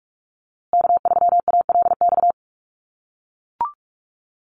Morse code, heard as a keyed tone, about 44 wpm, 690 Hz. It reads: K3ALX
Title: USA Call Signs Repeated 3x 45wpm